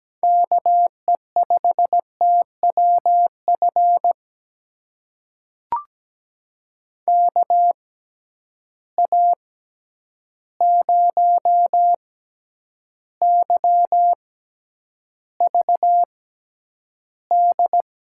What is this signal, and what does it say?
Morse code, heard as a keyed tone, about 17 wpm, spaced as for 9 wpm, 700 Hz. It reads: KE5TWF KA0YVD
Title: USA Call Signs - 6 Characters - 6x Character Spacing - 17wpm